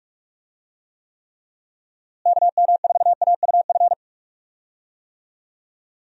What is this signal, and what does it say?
Morse code, heard as a keyed tone, about 46 wpm, 700 Hz. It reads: KM4AUF
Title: USA Call Signs - 6 Characters - Repeated 3x - 45wpm